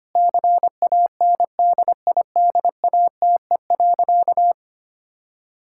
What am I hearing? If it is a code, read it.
Morse code, heard as a keyed tone, about 25 wpm, 700 Hz. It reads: CANDIDATE.
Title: Sentences from Top 1600 Words 25wpm